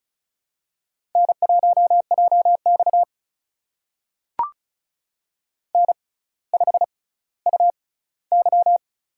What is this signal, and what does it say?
Morse code, heard as a keyed tone, about 36 wpm, spaced as for 18 wpm, 700 Hz. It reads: N1JX N5UY
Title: USA Call Signs - 4 Characters - 6x Character Spacing - 35wpm